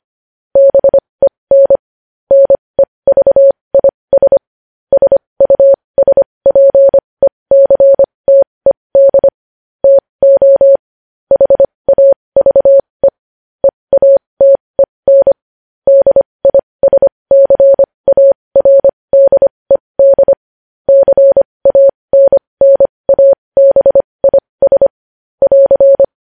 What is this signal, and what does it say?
Morse code, heard as a keyed tone, about 25 wpm, spaced as for 20 wpm, 550 Hz. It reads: BEN NEVIS SUSPECTED TO HAVE EATEN DISCARDED CANNABIS <AR>
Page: news-headlines-in-morse-code-at-20-wpm